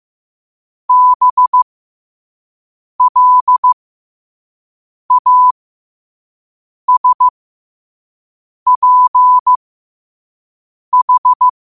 Morse code, heard as a keyed tone, about 15 wpm, spaced as for 5 wpm, 1000 Hz. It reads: BLASPH
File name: qotdfiles / 5wpm